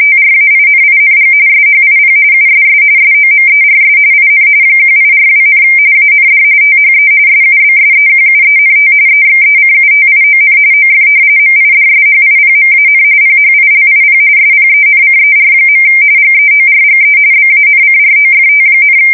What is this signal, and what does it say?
RTTY, about 45.45 baud. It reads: TRAINING SHIP USS SOMERS IN 1842, AND THE SUBSEQUENT EXECUTION OF MIDSHIPMAN PHILIP SPENCER. GEORGE BANCROFT,
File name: fsk170 / usnavy2